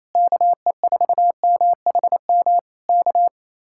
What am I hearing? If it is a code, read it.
Morse code, heard as a keyed tone, about 28 wpm, 700 Hz. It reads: KE4MHM K
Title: Speed-Racing Common QSO Phrases - 28wpm